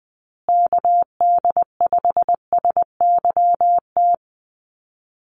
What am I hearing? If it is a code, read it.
Morse code, heard as a keyed tone, about 20 wpm, 700 Hz. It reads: KD5SYT
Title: USA Call Signs Repeated 3x 20wpm